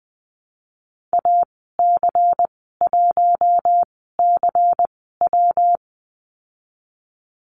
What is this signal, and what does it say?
Morse code, heard as a keyed tone, about 20 wpm, spaced as for 15 wpm, 700 Hz. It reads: AC1CW
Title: 122 Lesson 20g - Number 3 - Call Signs 20wpm